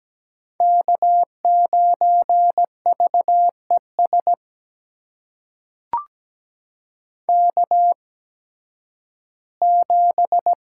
Morse code, heard as a keyed tone, about 17 wpm, spaced as for 11 wpm, 700 Hz. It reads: K9VES K7
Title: USA Call Signs - 5 Characters - 8x Character Spacing - 17wpm